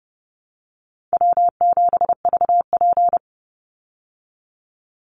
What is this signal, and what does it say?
Morse code, heard as a keyed tone, about 30 wpm, 700 Hz. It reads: W7VP
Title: Call Signs Repeated 3x - 30wpm